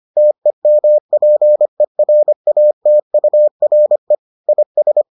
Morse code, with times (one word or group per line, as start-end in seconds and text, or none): 0.2-4.2 TEMPERATURE
4.5-5.0 IS